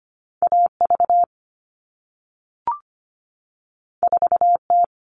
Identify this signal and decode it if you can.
Morse code, heard as a keyed tone, about 25 wpm, 700 Hz. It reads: AV 4T
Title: All Cut Numbers - 2-Digit 25wpm